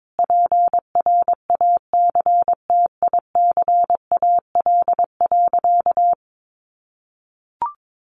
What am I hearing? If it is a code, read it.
Morse code, heard as a keyed tone, about 22 wpm, 700 Hz. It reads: PRACTICAL.
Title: Sentences from Top 1400 Words 22wpm